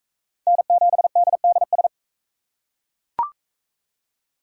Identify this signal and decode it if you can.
Morse code, heard as a keyed tone, about 38 wpm, 700 Hz. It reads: N7DDS